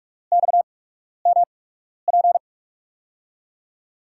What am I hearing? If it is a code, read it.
Morse code, heard as a keyed tone, about 43 wpm, spaced as for 12 wpm, 700 Hz. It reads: XMP
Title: USA Call Signs - 5 Characters - 8x Character Spacing - 45wpm